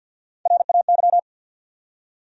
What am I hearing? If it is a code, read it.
Morse code, heard as a keyed tone, about 52 wpm, 700 Hz. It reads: RAY
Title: USA Common Names - Max 3 Letters - 50wpm